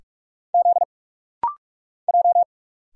Morse code, heard as a keyed tone, about 46 wpm, 700 Hz. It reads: Z J
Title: Single Letter-Number Alt - Rapid-Fire 45wpm